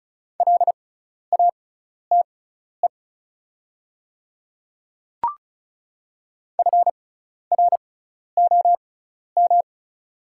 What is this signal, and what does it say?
Morse code, heard as a keyed tone, about 34 wpm, spaced as for 12 wpm, 690 Hz. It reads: LATE FROM
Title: 6x Character Spacing - Top 500 Words - 35wpm